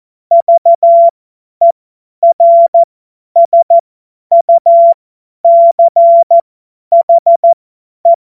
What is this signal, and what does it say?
Morse code, heard as a keyed tone, about 14 wpm, spaced as for 10 wpm, 680 Hz. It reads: VERSUCHE